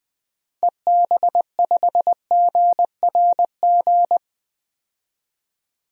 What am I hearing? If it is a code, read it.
Morse code, heard as a keyed tone, about 20 wpm, 700 Hz. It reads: EB5GRG
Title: DXCC-100 Call Signs - 6 Characters - 20wpm